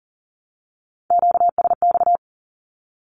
Morse code, heard as a keyed tone, about 40 wpm, 700 Hz. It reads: QSX
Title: Q-Codes 40wpm